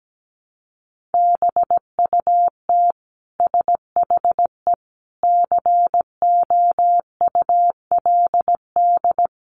Morse code, heard as a keyed tone, about 17 wpm, 700 Hz. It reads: BUT SHE COULD
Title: Sentences from Top 1200 Words 17wpm